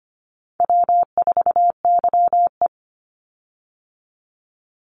Morse code, heard as a keyed tone, about 25 wpm, 700 Hz. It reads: W4YE
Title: Easy Call Signs - 25wpm